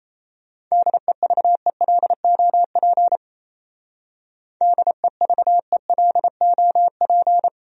Morse code, heard as a keyed tone, about 34 wpm, 700 Hz. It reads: DEVELOP DEVELOP
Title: Speed-Racing Top 500 Words 28wpm